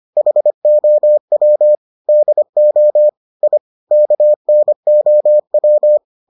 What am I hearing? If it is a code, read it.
Morse code, heard as a keyed tone, about 25 wpm, 600 Hz. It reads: HOW DO I KNOW